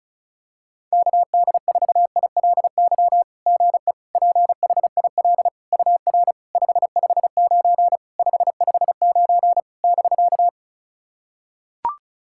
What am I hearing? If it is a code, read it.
Morse code, heard as a keyed tone, about 36 wpm, 700 Hz. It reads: KD4ILY GE PHIL UR 559 559 <BK>
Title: SOTA QSO Practice 35wpm